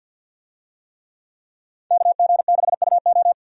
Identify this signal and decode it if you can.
Morse code, heard as a keyed tone, about 49 wpm, 700 Hz. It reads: KG6UO
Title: USA Call Signs - 5 Characters - 50wpm